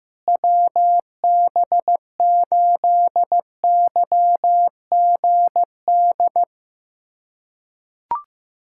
Morse code, heard as a keyed tone, about 15 wpm, 700 Hz. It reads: WB8YGD
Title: USA Callsign DE Callsign 15wpm